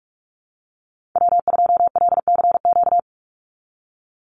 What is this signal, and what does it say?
Morse code, heard as a keyed tone, about 44 wpm, 700 Hz. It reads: W2LCQ
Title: Call Signs Repeated 2x - 45wpm